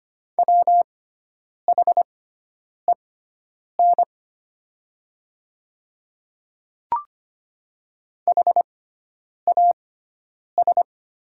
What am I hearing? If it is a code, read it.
Morse code, heard as a keyed tone, about 25 wpm, spaced as for 9 wpm, 700 Hz. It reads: WHEN HAS
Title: 6x Character Spacing - Top 200 Words - 25wpm